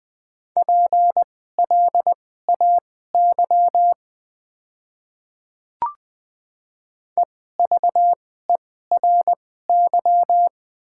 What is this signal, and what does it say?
Morse code, heard as a keyed tone, about 20 wpm, spaced as for 14 wpm, 700 Hz. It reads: PLAY EVERY
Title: 2x Character Spacing - Top 200 Words - 20wpm